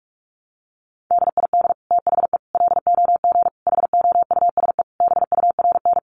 Morse code, heard as a keyed tone, about 44 wpm, 700 Hz. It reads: DID THE LOG HOUSE BURN
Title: Sentences from Top 1400 Words 45wpm